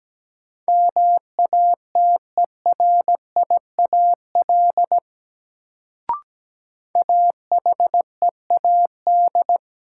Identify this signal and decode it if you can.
Morse code, heard as a keyed tone, about 17 wpm, 700 Hz. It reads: MATERIAL AHEAD
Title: Sentences from Top 1100 Words - New Word Drill Alt - 17wpm